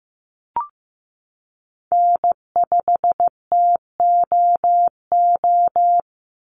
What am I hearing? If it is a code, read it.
Morse code, heard as a keyed tone, about 15 wpm, 700 Hz. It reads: N5TOO
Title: Call Signs Repeated 2x - 15wpm